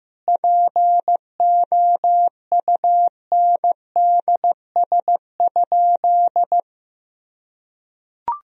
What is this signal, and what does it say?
Morse code, heard as a keyed tone, about 16 wpm, 700 Hz. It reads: POUNDS?
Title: Sentences from Top 1000 Words - Review Part 2 - 15wpm